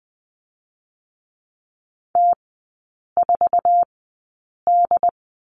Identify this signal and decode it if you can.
Morse code, heard as a keyed tone, about 20 wpm, spaced as for 8 wpm, 700 Hz. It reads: T4D
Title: 133 Lesson 22e - Question Mark - Three Characters 20wpm